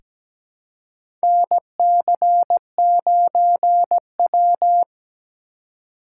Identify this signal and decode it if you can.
Morse code, heard as a keyed tone, about 17 wpm, 700 Hz. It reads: NC9W
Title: K1USN SST QSO Practice - 17wpm